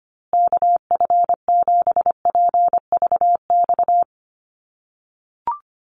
Morse code, heard as a keyed tone, about 25 wpm, 700 Hz. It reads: KF7PVX
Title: USA Call Signs Repeated 3x 25wpm